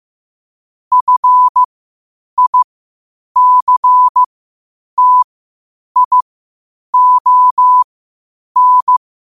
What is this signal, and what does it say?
Morse code, heard as a keyed tone, about 15 wpm, spaced as for 8 wpm, 1000 Hz. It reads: FICTION
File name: qotdfiles / 8wpm